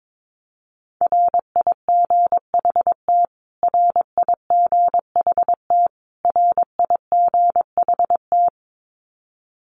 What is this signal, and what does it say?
Morse code, heard as a keyed tone, about 22 wpm, 700 Hz. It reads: RIGHT RIGHT RIGHT
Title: Top 500 Words Repeated 22wpm